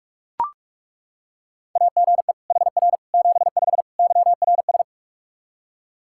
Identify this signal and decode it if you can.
Morse code, heard as a keyed tone, about 41 wpm, 700 Hz. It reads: AGE HR 75 YRS
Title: K1USN SST QSO Practice 45wpm